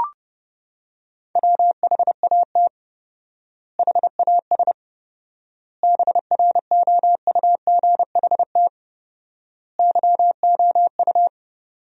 Morse code, heard as a keyed tone, about 30 wpm, 700 Hz. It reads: WHAT HAS BROUGHT YOU
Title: 4x Word-Spacing - Sentences from Top 500 Words - 30wpm